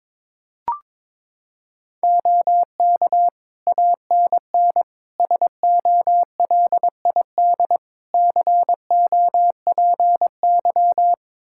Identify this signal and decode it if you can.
Morse code, heard as a keyed tone, about 22 wpm, 700 Hz. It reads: OK ANN SOLID COPY
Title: K1USN SST QSO Practice 22wpm